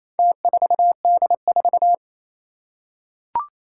Morse code, heard as a keyed tone, about 27 wpm, 700 Hz. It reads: T4D4
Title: Four Challenging Letter-Number Combinations 28wpm